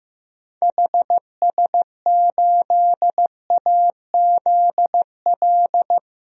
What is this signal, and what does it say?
Morse code, heard as a keyed tone, about 15 wpm, 700 Hz. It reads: HS8AZL
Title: DXCC-100 Call Signs - 6 Characters - Repeated 2x - 15wpm